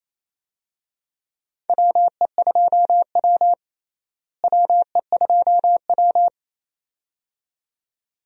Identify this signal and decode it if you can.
Morse code, heard as a keyed tone, about 28 wpm, 700 Hz. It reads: WE2W WE2W
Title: USA Call Signs - 4 Characters - Repeated 2x - 28wpm